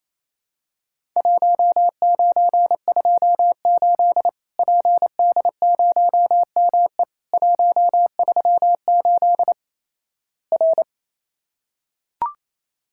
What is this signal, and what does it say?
Morse code, heard as a keyed tone, about 28 wpm, 700 Hz. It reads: 1928 PD0ME 138 R
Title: Worked All Europe DX Contest - QTC Exchange - 28wpm